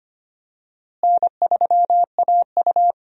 Morse code, heard as a keyed tone, about 25 wpm, 700 Hz. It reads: N3AU
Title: USA Call Signs - 4 Characters - Repeated 3x - 25wpm